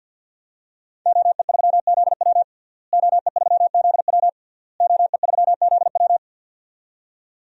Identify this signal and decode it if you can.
Morse code, heard as a keyed tone, about 51 wpm, 700 Hz. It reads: OE3ZW OE3ZW OE3ZW
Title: DXCC-100 Call Signs - 5 Characters - Repeated 3x - 50wpm